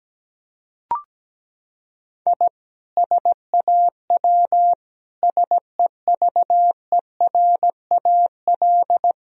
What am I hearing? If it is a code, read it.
Morse code, heard as a keyed tone, about 17 wpm, 700 Hz. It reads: I SAW SEVERAL
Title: Sentences from Top 1000 Words - Review Part 2 - 17wpm